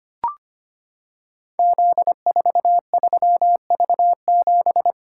Morse code, heard as a keyed tone, about 25 wpm, 700 Hz. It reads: Z43V7
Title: Code Groups - 5 Letter:Numbers - 25wpm